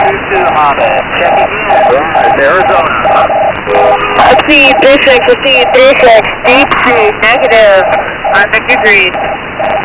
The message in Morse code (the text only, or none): none